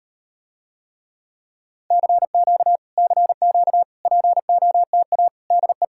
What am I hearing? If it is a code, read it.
Morse code, heard as a keyed tone, about 38 wpm, 700 Hz. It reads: CQ CQ POTA DE